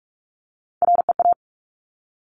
Morse code, heard as a keyed tone, about 46 wpm, 700 Hz. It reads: REA
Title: Top Three Letter Combinations in Words 45wpm